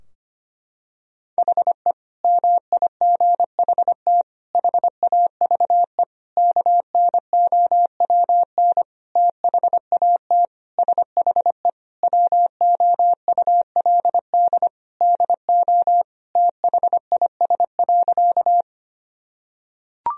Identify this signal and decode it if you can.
Morse code, heard as a keyed tone, about 25 wpm, 700 Hz. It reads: HE MIGHT HAVE KNOWN THAT SHE WOULD DO THIS.
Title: Sentences from Top 1000 Words - Review Part 3 - 25wpm